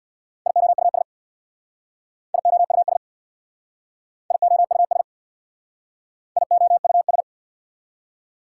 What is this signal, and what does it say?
Morse code, heard as a keyed tone, about 67 wpm, 690 Hz. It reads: IOUS IOUS IOUS IOUS